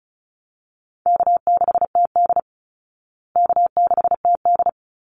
Morse code, heard as a keyed tone, about 34 wpm, 700 Hz. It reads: K6TD K6TD
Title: Call Signs Repeated 3x - 35wpm